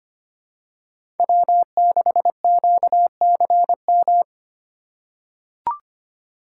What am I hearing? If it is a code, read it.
Morse code, heard as a keyed tone, about 25 wpm, 700 Hz. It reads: W6QCM
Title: USA Call Signs - 5 Characters - 2x Character Spacing - 25wpm